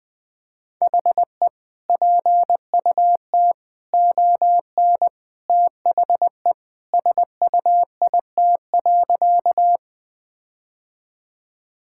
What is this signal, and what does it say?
Morse code, heard as a keyed tone, about 20 wpm, 700 Hz. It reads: HE PUT ON THE SUIT.
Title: Sentences from Top 1600 Words 20wpm